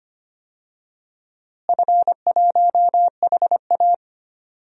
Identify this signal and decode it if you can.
Morse code, heard as a keyed tone, about 25 wpm, 700 Hz. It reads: F1HA
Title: DXCC-100 Call Signs - 4 Characters - Repeated 2x - 25wpm